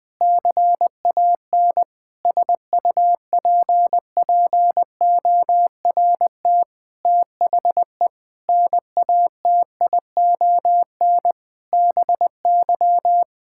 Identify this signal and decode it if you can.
Morse code, heard as a keyed tone, about 20 wpm, 700 Hz. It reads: CAN SUPPORT THE NATION BY
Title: Sentences with 9-Word Phrases - Spoken Beginning - 20wpm